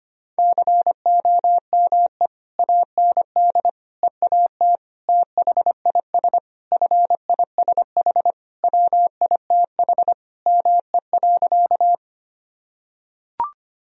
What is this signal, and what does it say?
Morse code, heard as a keyed tone, about 25 wpm, 700 Hz. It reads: COME AND EAT THIS FISH WITH ME.
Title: Sentences from Top 1000 Words - Review Part 1 - 25wpm